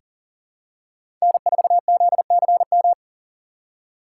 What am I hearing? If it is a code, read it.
Morse code, heard as a keyed tone, about 38 wpm, 690 Hz. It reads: N4ZCM
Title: USA Call Signs - 2x Character Spacing - 40wpm